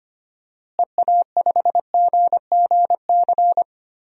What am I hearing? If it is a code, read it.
Morse code, heard as a keyed tone, about 25 wpm, 700 Hz. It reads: EA5GGC
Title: DXCC-100 Call Signs - 6 Characters - Repeated 2x - 25wpm